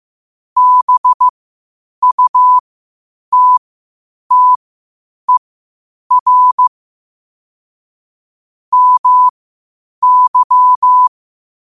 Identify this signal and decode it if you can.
Morse code, heard as a keyed tone, about 15 wpm, spaced as for 8 wpm, 1000 Hz. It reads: BUTTER MY
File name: qotdfiles / 8wpm